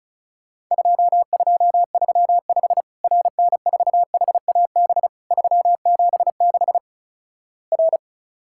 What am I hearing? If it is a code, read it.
Morse code, heard as a keyed tone, about 36 wpm, 700 Hz. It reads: 2235 RN4HAB 376 R
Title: Worked All Europe DX Contest - QTC Exchange - 35wpm